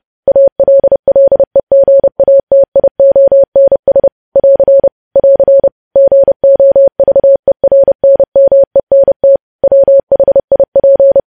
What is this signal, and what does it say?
Morse code, heard as a keyed tone, about 30 wpm, 550 Hz. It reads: ALLEGATIONS <AR> <AR> GOVERNMENT WHIP
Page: news headlines in morse code at 30 wpm